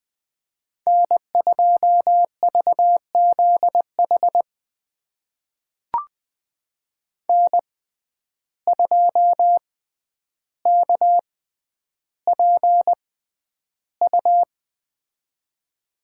Morse code, heard as a keyed tone, about 20 wpm, spaced as for 10 wpm, 700 Hz. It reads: N2VZH N2KPU
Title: USA Call Signs - 6x Character Spacing - 20wpm